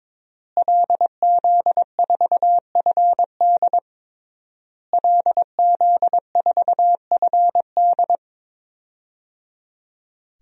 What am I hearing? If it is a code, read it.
Morse code, heard as a keyed tone, about 21 wpm, 700 Hz. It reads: LZ4FD LZ4FD
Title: DXCC-100 Call Signs - 5 Characters - Repeated 3x - 22wpm